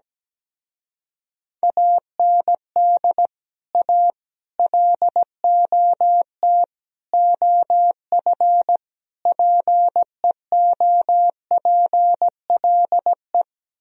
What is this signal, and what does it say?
Morse code, heard as a keyed tone, about 17 wpm, 700 Hz. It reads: AND A LOT OF PEOPLE
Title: Speed-Racing - Sets of 5 Words 17wpm